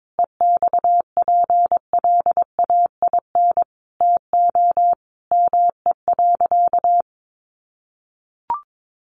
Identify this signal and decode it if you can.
Morse code, heard as a keyed tone, about 22 wpm, 700 Hz. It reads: EXPLAIN TO ME.
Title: Sentences from Top 1100 Words 22wpm